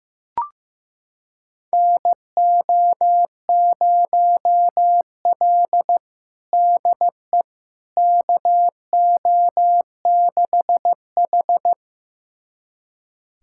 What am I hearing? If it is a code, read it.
Morse code, heard as a keyed tone, about 15 wpm, 700 Hz. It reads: NO0L DE KO6H
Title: USA Callsign DE Callsign - 4 Characters - 15wpm